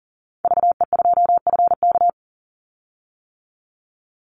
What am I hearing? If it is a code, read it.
Morse code, heard as a keyed tone, about 39 wpm, 700 Hz. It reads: VE2FK
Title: Call Signs - 40wpm